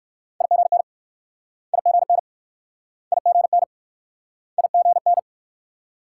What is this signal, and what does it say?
Morse code, heard as a keyed tone, about 64 wpm, 700 Hz. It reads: IGN IGN IGN IGN